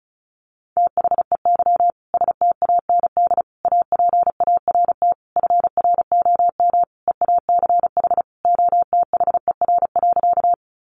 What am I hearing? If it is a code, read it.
Morse code, heard as a keyed tone, about 34 wpm, 700 Hz. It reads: THEY STAND APART FROM EACH OTHER.
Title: Easy Phrases 35wpm